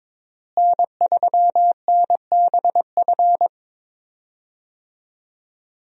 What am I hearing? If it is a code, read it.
Morse code, heard as a keyed tone, about 22 wpm, 700 Hz. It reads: N3NBF
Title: USA Call Signs - 5 Characters - Repeated 2x - 22wpm